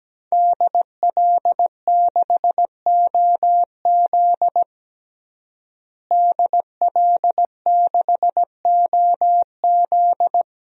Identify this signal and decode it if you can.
Morse code, heard as a keyed tone, about 17 wpm, 700 Hz. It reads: DL6OZ DL6OZ
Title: DXCC-100 Call Signs - 5 Characters - Repeated 2x - 17wpm